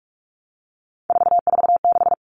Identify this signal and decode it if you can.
Morse code, heard as a keyed tone, about 44 wpm, 700 Hz. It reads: 446
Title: Numbers - 3-Digit 45wpm